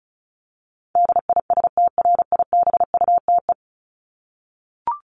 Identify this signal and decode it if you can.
Morse code, heard as a keyed tone, about 34 wpm, 700 Hz. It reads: DISTRIBUTE
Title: Top 1501-1750 Words 35wpm